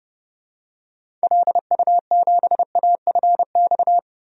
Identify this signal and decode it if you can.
Morse code, heard as a keyed tone, about 32 wpm, 700 Hz. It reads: LU7AFX
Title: DXCC-100 Call Signs - 6 Characters - 30wpm